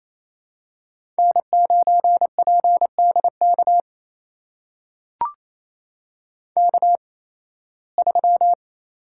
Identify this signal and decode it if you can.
Morse code, heard as a keyed tone, about 28 wpm, spaced as for 18 wpm, 700 Hz. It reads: N9PDK K3
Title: USA Call Signs - 5 Characters - 8x Character Spacing - 28wpm